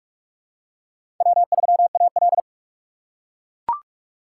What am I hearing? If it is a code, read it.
Morse code, heard as a keyed tone, about 41 wpm, 700 Hz. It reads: W3AL
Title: USA Call Signs - 4 Characters - 4x Character Spacing - 45wpm